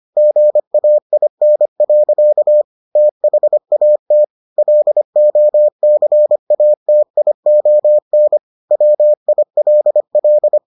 Morse code, heard as a keyed tone, about 25 wpm, 600 Hz. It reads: GAIN. THAT LOCATION WILL